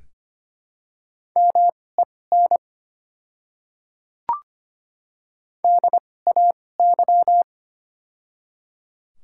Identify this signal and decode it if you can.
Morse code, heard as a keyed tone, about 25 wpm, spaced as for 18 wpm, 700 Hz. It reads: MEN DAY
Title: 2x Character Spacing - Top 200 Words - 25wpm